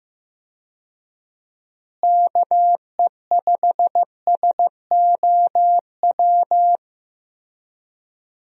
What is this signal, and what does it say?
Morse code, heard as a keyed tone, about 15 wpm, 700 Hz. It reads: KE5SOW
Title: USA Call Signs - 6 Characters - Repeated 3x - 15wpm